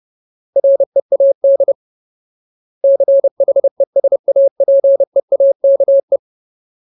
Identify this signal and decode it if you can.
Morse code, heard as a keyed tone, about 30 wpm, 550 Hz. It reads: READ CHESAPEAKE
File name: Ultimate Word List / 30WPM Ultimate